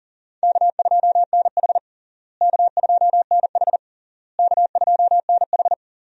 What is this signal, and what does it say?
Morse code, heard as a keyed tone, about 40 wpm, 700 Hz. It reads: K2NH K2NH K2NH
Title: USA Call Signs - 4 Characters - Repeated 3x - 40wpm